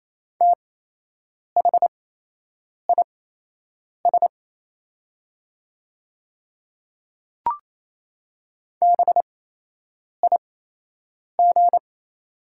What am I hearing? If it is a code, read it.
Morse code, heard as a keyed tone, about 28 wpm, spaced as for 8 wpm, 700 Hz. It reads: THIS BIG
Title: 8x Character Spacing - Top 200 Words - 28wpm